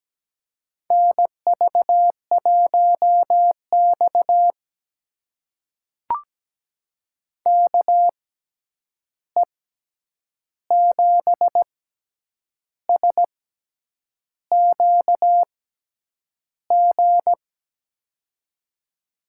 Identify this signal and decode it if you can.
Morse code, heard as a keyed tone, about 17 wpm, spaced as for 8 wpm, 700 Hz. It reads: NV1X KE7SQG
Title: USA Call Signs - 6x Character Spacing - 17wpm